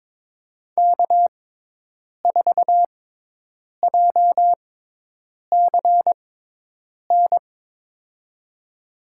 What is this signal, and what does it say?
Morse code, heard as a keyed tone, about 22 wpm, spaced as for 8 wpm, 700 Hz. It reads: K4JCN
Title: USA Call Signs - 5 Characters - 6x Character Spacing - 22wpm